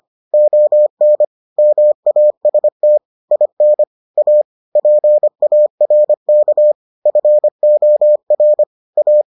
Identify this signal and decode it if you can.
Morse code, heard as a keyed tone, about 25 wpm, 600 Hz. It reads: ON MAST IN A PARK FOR A